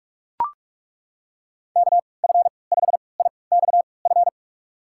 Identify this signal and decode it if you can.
Morse code, heard as a keyed tone, about 41 wpm, spaced as for 25 wpm, 700 Hz. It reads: KF5IXF
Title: USA Call Signs - 6 Characters - 3x Character Spacing - 45wpm